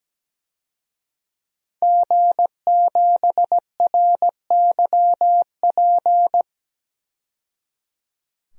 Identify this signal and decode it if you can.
Morse code, heard as a keyed tone, about 17 wpm, 700 Hz. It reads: G7RYP
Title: DXCC-100 Call Signs - 5 Characters - 17wpm